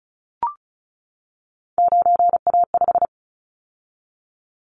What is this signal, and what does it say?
Morse code, heard as a keyed tone, about 34 wpm, 700 Hz. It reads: 9A5
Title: Callsign Prefixes for Top 100 DXCC - 35wpm